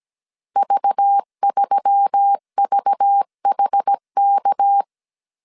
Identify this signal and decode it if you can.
Morse code, heard as a keyed tone, about 17 wpm, 800 Hz. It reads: V3VHK
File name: CW / DET / 1 6 1500C S A 16wpm 2635